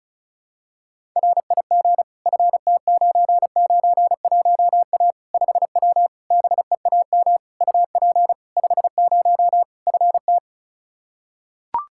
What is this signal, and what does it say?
Morse code, heard as a keyed tone, about 36 wpm, 700 Hz. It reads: RIG FT991A 5W BEAM UP 50 FT
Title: Practice QSO - Exchange 2 - Info - 35wpm